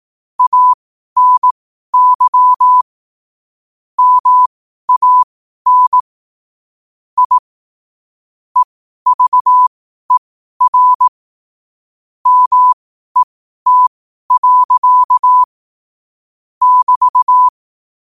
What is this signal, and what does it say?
Morse code, heard as a keyed tone, about 18 wpm, spaced as for 12 wpm, 1000 Hz. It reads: ANY MAN I EVER MET. <BT>
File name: qotdfiles / 12wpm